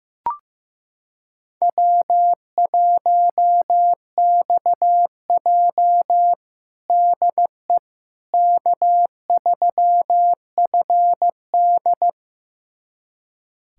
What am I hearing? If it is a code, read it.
Morse code, heard as a keyed tone, about 15 wpm, 700 Hz. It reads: W1XJ DE K3FD
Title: USA Callsign DE Callsign - 4 Characters - 15wpm